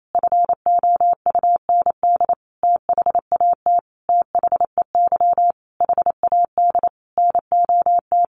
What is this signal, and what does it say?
Morse code, heard as a keyed tone, about 28 wpm, 700 Hz. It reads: FOUND THAT THEY HAD NOT
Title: Sentences from Top 1200 Words 28wpm